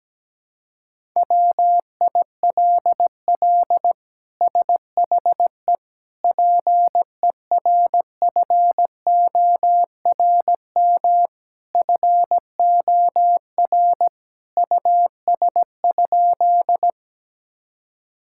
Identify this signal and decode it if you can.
Morse code, heard as a keyed tone, about 17 wpm, 700 Hz. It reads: WILL SHE PERFORM FOR US?
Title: Sentences from Top 1000 Words - Review Part 5 - 17wpm